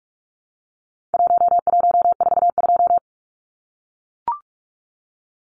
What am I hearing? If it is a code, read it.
Morse code, heard as a keyed tone, about 44 wpm, 700 Hz. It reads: 1142
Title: Numbers - Contest Style - 4-Digit 45wpm